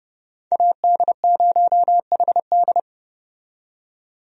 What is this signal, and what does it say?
Morse code, heard as a keyed tone, about 30 wpm, 700 Hz. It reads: AD0HD
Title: USA Call Signs - 5 Characters - 30wpm